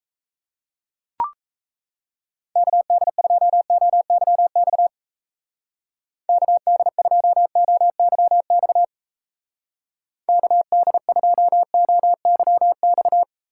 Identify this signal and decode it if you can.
Morse code, heard as a keyed tone, about 35 wpm, 700 Hz. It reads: KD2OYX KD2OYX KD2OYX
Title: Speed Racing - USA Call Signs 28wpm